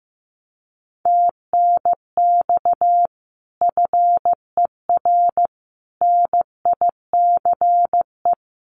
Morse code, heard as a keyed tone, about 15 wpm, 700 Hz. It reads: TNX FER NICE
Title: Common QSO Phrases Encore 15wpm